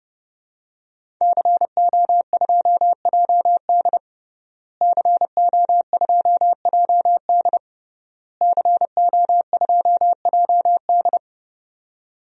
Morse code, heard as a keyed tone, about 30 wpm, 700 Hz. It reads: CO2JD CO2JD CO2JD
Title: DXCC-100 Call Signs - 5 Characters - Repeated 3x - 30wpm